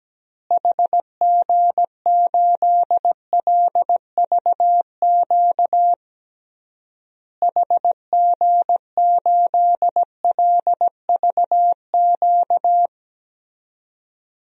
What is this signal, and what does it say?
Morse code, heard as a keyed tone, about 17 wpm, 700 Hz. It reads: HG8LVQ HG8LVQ
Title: DXCC-100 Call Signs - 6 Characters - Repeated 3x - 17wpm